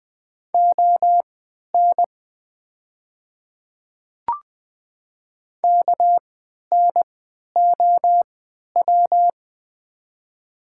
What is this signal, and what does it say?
Morse code, heard as a keyed tone, about 20 wpm, spaced as for 11 wpm, 700 Hz. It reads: ON KNOW
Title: 3x Character Spacing - Top 100 Words - 20wpm